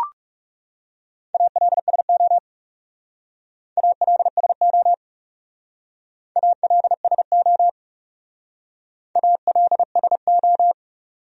Morse code, heard as a keyed tone, about 39 wpm, 700 Hz. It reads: ALSO ALSO ALSO ALSO